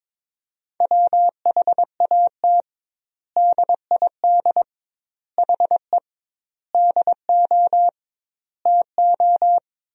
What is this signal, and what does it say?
Morse code, heard as a keyed tone, about 21 wpm, 700 Hz. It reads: WHAT DID HE DO TO
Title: 2x Word-Spacing - Sentences from Top 100 Words - 22wpm